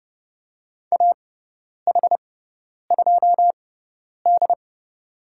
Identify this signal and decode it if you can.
Morse code, heard as a keyed tone, about 30 wpm, spaced as for 10 wpm, 700 Hz. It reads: AH2D
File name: Lesson 18g - Letter m - Call Signs 30wpm